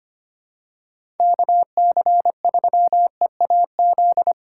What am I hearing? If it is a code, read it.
Morse code, heard as a keyed tone, about 25 wpm, 700 Hz. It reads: KC3EAZ
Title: USA Call Signs - 6 Characters - 4x Character Spacing - 25wpm